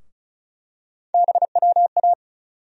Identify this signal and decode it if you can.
Morse code, heard as a keyed tone, about 34 wpm, 700 Hz. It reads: BWA
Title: USA Call Sign Postfixes - 3 Letters 35wpm